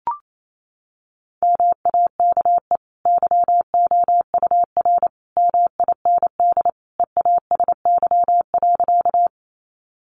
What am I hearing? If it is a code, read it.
Morse code, heard as a keyed tone, about 28 wpm, 700 Hz. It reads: MAKE YOUR MIND EASY.
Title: Sentences from Top 700 Words 28wpm